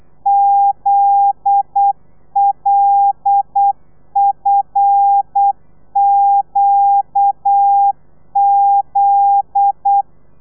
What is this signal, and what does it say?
Morse code, heard as a keyed tone, about 8 wpm, 800 Hz. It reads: ZLFQZ
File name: telegrafia / i2tfj / Lez9 35